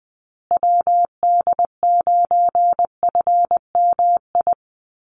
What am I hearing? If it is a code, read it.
Morse code, heard as a keyed tone, about 20 wpm, 700 Hz. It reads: WD9FMI
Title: USA Call Signs Repeated 3x 20wpm